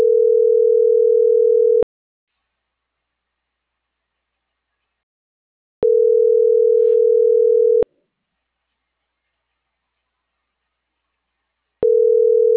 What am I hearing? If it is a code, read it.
Ringback tone (North America), 440 + 480 Hz.